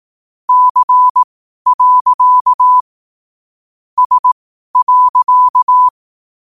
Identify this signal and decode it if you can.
Morse code, heard as a keyed tone, about 18 wpm, spaced as for 12 wpm, 1000 Hz. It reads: C. S.
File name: qotdfiles / 12wpm